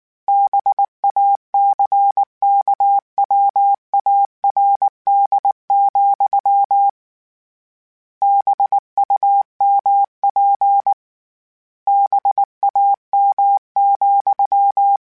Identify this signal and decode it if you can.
Morse code, heard as a keyed tone, about 19 wpm, 800 Hz. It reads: BACKWARD, BUMP BAM,
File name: Misc / Misc-mp3s / BIG @18